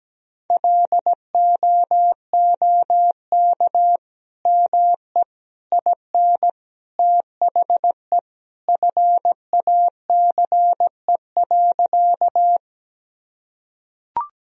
Morse code, heard as a keyed tone, about 17 wpm, 700 Hz. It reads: LOOK ME IN THE FACE.
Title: Sentences from Top 1000 Words - Review Part 2 - 17wpm